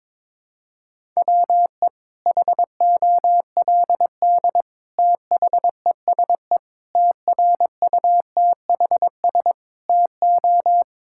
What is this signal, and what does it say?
Morse code, heard as a keyed tone, about 22 wpm, 700 Hz. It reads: WE HOLD THESE TRUTHS TO
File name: Sets of 9 Words - 22wpm